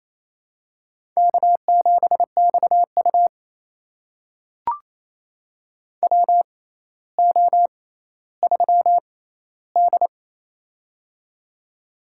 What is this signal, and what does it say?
Morse code, heard as a keyed tone, about 28 wpm, spaced as for 14 wpm, 700 Hz. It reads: K7XU WO3D
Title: USA Call Signs - 6x Character Spacing - 28wpm